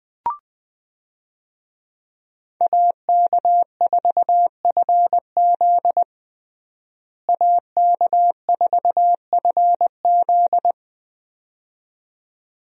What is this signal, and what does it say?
Morse code, heard as a keyed tone, about 21 wpm, 700 Hz. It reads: AK4FZ AK4FZ
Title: USA Call Signs - 5 Characters - Repeated 2x - 20wpm